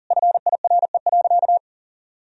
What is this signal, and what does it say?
Morse code, heard as a keyed tone, about 37 wpm, 700 Hz. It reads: FIRE.
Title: Sentences from Top 1000 Words - Review Part 1 - 40wpm